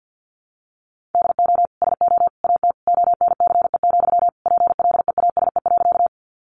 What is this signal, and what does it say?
Morse code, heard as a keyed tone, about 50 wpm, 700 Hz. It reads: DO SO AT ONCE, PLEASE.
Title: Sentences from Top 1500 Words 50wpm